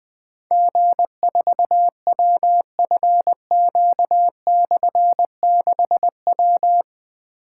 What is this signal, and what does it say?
Morse code, heard as a keyed tone, about 20 wpm, 700 Hz. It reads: G4WFQ/6W
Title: Difficult Call Signs 20wpm